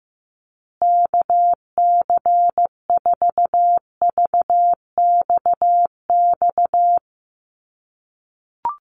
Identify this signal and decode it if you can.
Morse code, heard as a keyed tone, about 15 wpm, 700 Hz. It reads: KC4VXX
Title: USA Call Signs Repeated 2x 15wpm